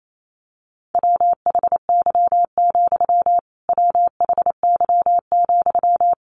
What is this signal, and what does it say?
Morse code, heard as a keyed tone, about 28 wpm, 700 Hz. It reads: WHY, WHY,
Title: A Sound Of Thunder 28wpm